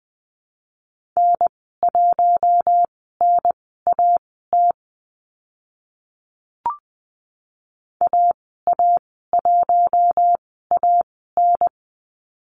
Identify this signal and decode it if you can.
Morse code, heard as a keyed tone, about 20 wpm, spaced as for 14 wpm, 700 Hz. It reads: N1NAT AA1AN
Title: 048 Lesson 9g - Number 4 - Call Signs 20wpm